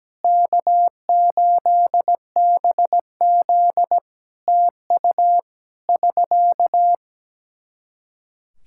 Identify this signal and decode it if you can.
Morse code, heard as a keyed tone, about 17 wpm, 700 Hz. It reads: K8BZ TU <SK>
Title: Practice QSO - Exchange 4 - Ending - 17wpm